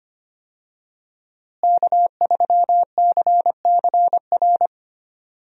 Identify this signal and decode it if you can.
Morse code, heard as a keyed tone, about 25 wpm, 700 Hz. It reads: K3CCR
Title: USA Call Signs - 5 Characters - Repeated 3x - 25wpm